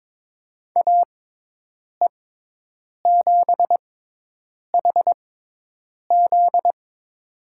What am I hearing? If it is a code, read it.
Morse code, heard as a keyed tone, about 22 wpm, spaced as for 8 wpm, 700 Hz. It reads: AE7HZ
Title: USA Call Signs - 6x Character Spacing - 22wpm